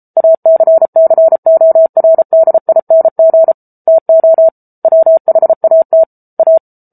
Morse code, heard as a keyed tone, about 34 wpm, 650 Hz. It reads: ACCORDING TO WHAT A